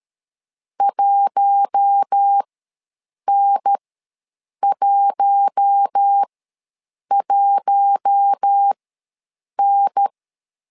Morse code, heard as a keyed tone, about 13 wpm, spaced as for 7 wpm, 800 Hz. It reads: 1N11N
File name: LEZIONI / 4 Q1 ws 12wpm 2129